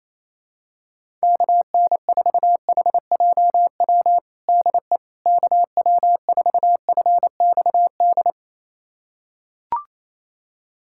Morse code, heard as a keyed tone, about 28 wpm, 700 Hz. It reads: KN4HJW DE KW4FXD
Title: USA Callsign DE Callsign - 6 Characters - 28wpm